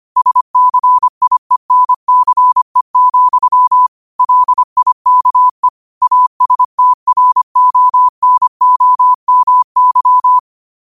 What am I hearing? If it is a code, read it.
Morse code, heard as a keyed tone, about 25 wpm, 1000 Hz. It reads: SCIENCE, LIKE ASTRONOMY